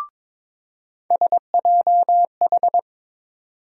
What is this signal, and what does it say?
Morse code, heard as a keyed tone, about 21 wpm, 700 Hz. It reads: SJH
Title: USA Call Sign Postfixes - 3 Letters 22wpm